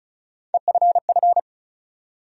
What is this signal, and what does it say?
Morse code, heard as a keyed tone, about 36 wpm, 700 Hz. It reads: EFF
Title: USA Call Sign Postfixes - 3 Letters 35wpm